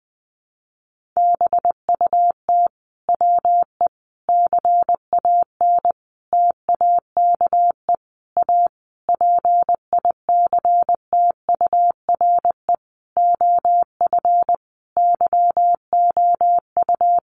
Morse code, heard as a keyed tone, about 20 wpm, 700 Hz. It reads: BUT WE CAN TAKE A PICTURE OF YOU
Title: A Sound Of Thunder 20wpm